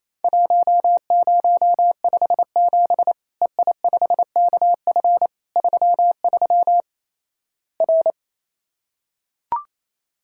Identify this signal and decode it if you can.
Morse code, heard as a keyed tone, about 28 wpm, 700 Hz. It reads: 1057 EI5KF 33 R
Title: Worked All Europe DX Contest - QTC Exchange - 28wpm